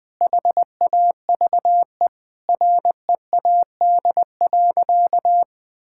Morse code, heard as a keyed tone, about 21 wpm, 700 Hz. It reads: HAVE READ.
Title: Sentences with 5-Word Phrases - Spoken Beginning - 20wpm